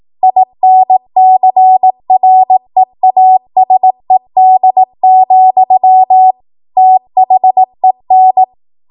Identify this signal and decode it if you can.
Morse code, heard as a keyed tone, about 18 wpm, 750 Hz. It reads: INCREASED, THEN
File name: Morse / 18WPM